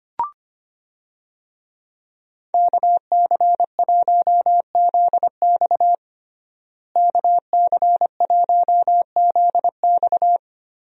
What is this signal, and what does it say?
Morse code, heard as a keyed tone, about 25 wpm, 690 Hz. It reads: KC1ZX KC1ZX
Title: USA Call Signs - 5 Characters - Repeated 2x - 25wpm